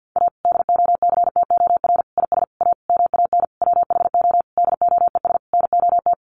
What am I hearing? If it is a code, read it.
Morse code, heard as a keyed tone, about 50 wpm, 700 Hz. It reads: A DOCTOR IS A MAN WHO DOES NOT
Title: Sentences from Top 1500 Words 50wpm